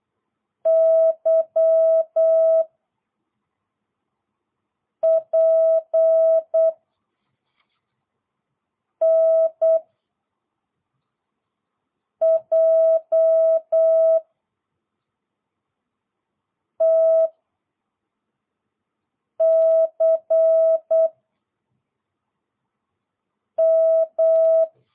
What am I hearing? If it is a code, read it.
Morse code, heard as a keyed tone, about 8 wpm, 650 Hz. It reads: Y P N J T C M